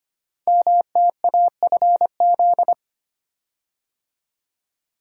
Morse code, heard as a keyed tone, about 25 wpm, 700 Hz. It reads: MTAFZ
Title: Code Groups - 5 Letters - 25wpm